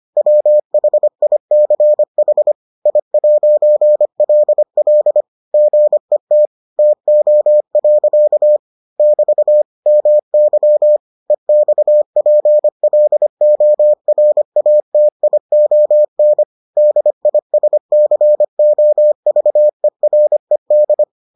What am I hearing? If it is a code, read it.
Morse code, heard as a keyed tone, about 25 wpm, 600 Hz. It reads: WHICH I'LL GET TO. <BT> MY EXPLORATION DISCOVERED